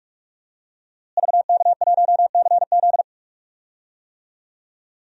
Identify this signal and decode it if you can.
Morse code, heard as a keyed tone, about 44 wpm, 700 Hz. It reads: VK1CZ
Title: DXCC-100 Call Signs - 5 Characters - Repeated 3x - 45wpm